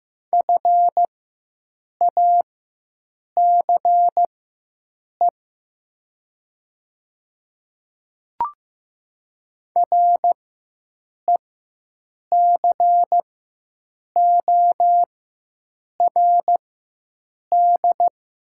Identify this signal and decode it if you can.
Morse code, heard as a keyed tone, about 15 wpm, spaced as for 7 wpm, 700 Hz. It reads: FACE RECORD
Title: 4x Character Spacing - Top 500 Words - 15wpm